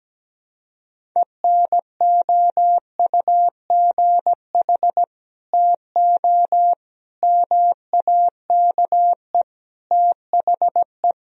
Morse code, heard as a keyed tone, about 17 wpm, 700 Hz. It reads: ENOUGH TO MAKE THE
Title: Sentences from Top 1000 Words - Review Part 1 - 17wpm